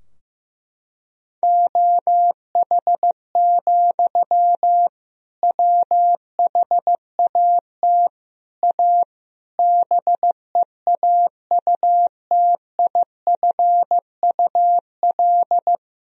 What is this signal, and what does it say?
Morse code, heard as a keyed tone, about 15 wpm, 700 Hz. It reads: OH, WHAT A BEAUTIFUL
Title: Sentences from Top 1000 Words - Review Part 2 - 15wpm